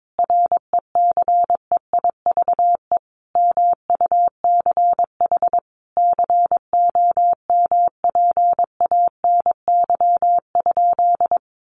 Morse code, heard as a keyed tone, about 22 wpm, 700 Hz. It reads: RECEIVE MUCH COMPANY?
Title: Sentences from Top 1000 Words 22wpm